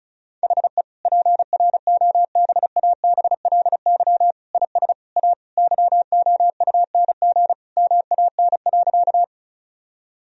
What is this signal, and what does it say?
Morse code, heard as a keyed tone, about 34 wpm, 700 Hz. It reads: HE PROBABLY IS A YOUNG MAN.
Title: Sentences from Top 1000 Words - Review Part 4 - 35wpm